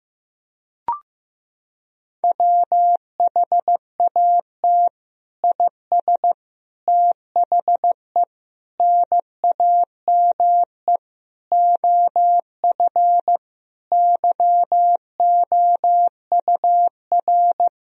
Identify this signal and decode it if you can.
Morse code, heard as a keyed tone, about 16 wpm, 700 Hz. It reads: WHAT IS THE NAME OF YOUR
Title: Sentences from Top 1000 Words - Review Part 2 - 15wpm